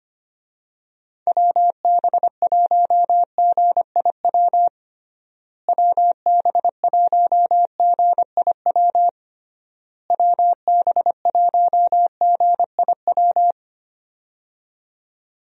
Morse code, heard as a keyed tone, about 25 wpm, 700 Hz. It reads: WB1GIW WB1GIW WB1GIW
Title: USA Call Signs - 6 Characters - Repeated 3x - 25wpm